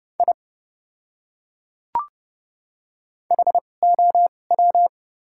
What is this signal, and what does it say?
Morse code, heard as a keyed tone, about 30 wpm, spaced as for 22 wpm, 700 Hz. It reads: I HOW
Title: 2x Character Spacing - Top 100 Words - 30wpm